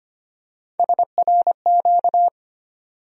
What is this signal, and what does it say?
Morse code, heard as a keyed tone, about 25 wpm, 700 Hz. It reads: SRQ
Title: USA Call Sign Postfixes - 3 Letters 25wpm